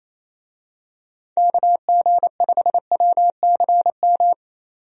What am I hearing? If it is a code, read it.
Morse code, heard as a keyed tone, about 28 wpm, 700 Hz. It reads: KG5WCM
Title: USA Call Signs - 6 Characters - 28wpm